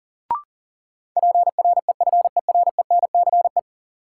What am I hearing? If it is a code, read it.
Morse code, heard as a keyed tone, about 38 wpm, 700 Hz. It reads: PREFERENCE
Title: Sentences from Top 1500 Words - New Word Drill Alt - 40wpm